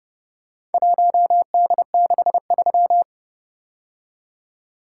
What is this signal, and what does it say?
Morse code, heard as a keyed tone, about 29 wpm, 700 Hz. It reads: 1D63